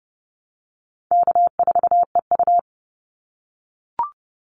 Morse code, heard as a keyed tone, about 30 wpm, 700 Hz. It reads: K4EU
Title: Call Signs - 30wpm